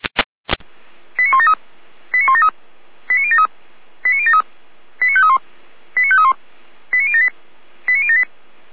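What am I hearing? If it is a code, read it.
Five-tone selective call (ZVEI), heard as quick runs of five tones, tones of about 70 ms. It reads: 89173 89173 89073 89073 89531 89531 89078 89078
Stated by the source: Five-tone ZVEI selective call received by SDR.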